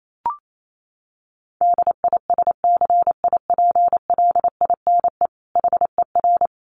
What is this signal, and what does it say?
Morse code, heard as a keyed tone, about 28 wpm, 700 Hz. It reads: DISCIPLINE HER
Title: Sentences from Top 900 Words 28wpm